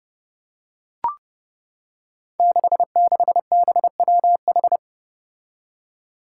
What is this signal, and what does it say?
Morse code, heard as a keyed tone, about 30 wpm, 700 Hz. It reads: 66BWH
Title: Five Challenging Letter-Number Combinations 30wpm